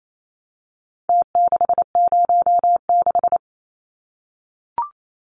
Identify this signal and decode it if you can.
Morse code, heard as a keyed tone, about 28 wpm, 700 Hz. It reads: T606
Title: Numbers - Contest Style - 4-Digit 28wpm